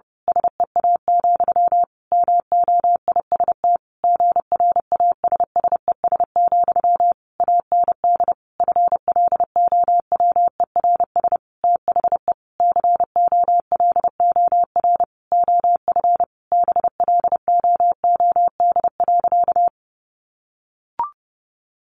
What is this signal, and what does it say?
Morse code, heard as a keyed tone, about 30 wpm, 700 Hz. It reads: SEA, MOIST GRASSES, AND FLOWERS THE COLOR OF BLOOD.
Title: A Sound Of Thunder 30wpm